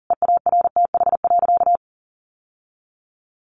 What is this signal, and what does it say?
Morse code, heard as a keyed tone, about 40 wpm, 700 Hz. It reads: EARTH.
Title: Sentences from Top 900 Words 40wpm